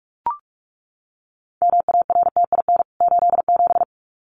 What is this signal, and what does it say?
Morse code, heard as a keyed tone, about 44 wpm, 700 Hz. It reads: MARTIN 87
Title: CWT Exchange 45wpm